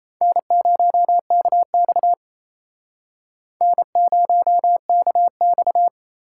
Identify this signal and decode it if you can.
Morse code, heard as a keyed tone, about 35 wpm, 700 Hz. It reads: N0KX N0KX
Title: Speed Racing - USA 1x2 Call Signs 28wpm